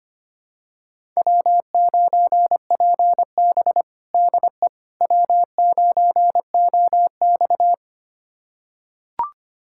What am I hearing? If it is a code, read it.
Morse code, heard as a keyed tone, about 25 wpm, 700 Hz. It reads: W9PB DE W9OX
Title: USA Callsign DE Callsign - 4 Characters - 25wpm